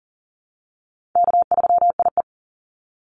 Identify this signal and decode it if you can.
Morse code, heard as a keyed tone, about 39 wpm, 700 Hz. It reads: K3IE
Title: Call Signs - 40wpm